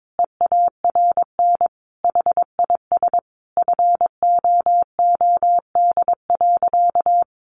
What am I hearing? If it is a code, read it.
Morse code, heard as a keyed tone, about 22 wpm, 700 Hz. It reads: EARN HIS FOOD.
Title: Sentences from Top 1500 Words 22wpm